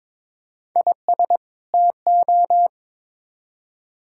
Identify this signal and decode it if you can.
Morse code, heard as a keyed tone, about 22 wpm, 700 Hz. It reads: IS TO
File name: Build A Phrase - 2-Word Phrases - 22wpm